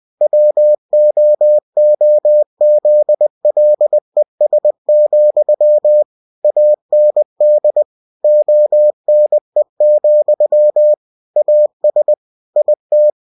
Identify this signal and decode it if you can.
Morse code, heard as a keyed tone, about 20 wpm, 600 Hz. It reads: WOOZLES, AND ONE, AS IT
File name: cw-books / winnie-the-pooh / winnie-20wpm-page-0023